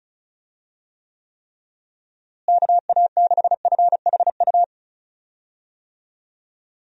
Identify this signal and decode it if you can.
Morse code, heard as a keyed tone, about 34 wpm, 700 Hz. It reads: KA6FHU
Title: USA Call Signs - 6 Characters - Repeated 3x - 35wpm